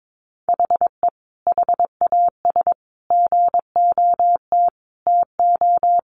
Morse code, heard as a keyed tone, about 22 wpm, 700 Hz. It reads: HE HAS GOT TO
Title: Sentences from Top 800 Words 22wpm